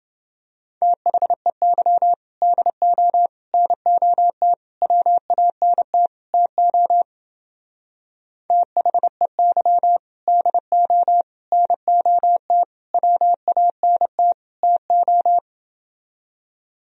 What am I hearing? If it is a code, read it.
Morse code, heard as a keyed tone, about 28 wpm, 700 Hz. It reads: THEY DO NOT WANT TO THEY DO NOT WANT TO